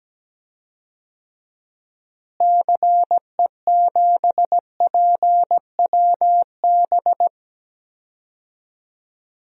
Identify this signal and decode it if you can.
Morse code, heard as a keyed tone, about 17 wpm, 700 Hz. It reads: CE7PWB
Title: DXCC-100 Call Signs - 6 Characters - Repeated 2x - 17wpm